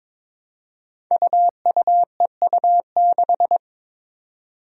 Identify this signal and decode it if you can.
Morse code, heard as a keyed tone, about 22 wpm, 700 Hz. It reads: UUEU6
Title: Five Challenging Letter-Number Combinations 22wpm